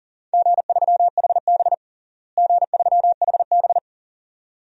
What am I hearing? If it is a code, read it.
Morse code, heard as a keyed tone, about 40 wpm, 700 Hz. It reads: G3HB G3HB
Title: DXCC-100 Call Signs - 4 Characters - Repeated 2x - 40wpm